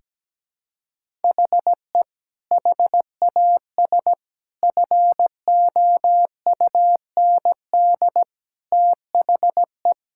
Morse code, heard as a keyed tone, about 17 wpm, 700 Hz. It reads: HE HAS FOUND THE
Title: Sentences from Top 1000 Words - Review Part 5 - 17wpm